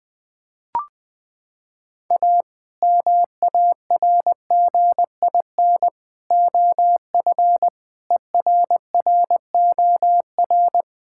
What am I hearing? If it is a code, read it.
Morse code, heard as a keyed tone, about 21 wpm, 700 Hz. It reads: A MARGIN OF ERROR
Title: Sets of 8 Words - 20wpm